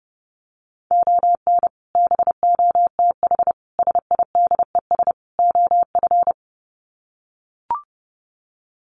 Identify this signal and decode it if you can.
Morse code, heard as a keyed tone, about 30 wpm, 700 Hz. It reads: ON BOTH SIDES OF
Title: Sets of 4 Words 30wpm